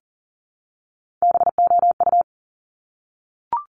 Morse code, heard as a keyed tone, about 42 wpm, 700 Hz. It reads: BOU
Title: Top Three Letter Combinations in Words 40wpm